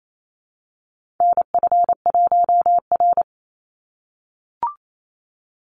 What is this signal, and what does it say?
Morse code, heard as a keyed tone, about 28 wpm, 700 Hz. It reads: NF1R
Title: NA SOTA Chaser Callsigns 28wpm